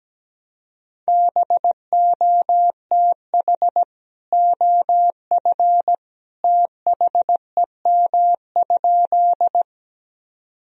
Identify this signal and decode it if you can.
Morse code, heard as a keyed tone, about 17 wpm, 700 Hz. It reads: BOTH OF THEM?
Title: Sentences from Top 1000 Words - Review Part 1 - 17wpm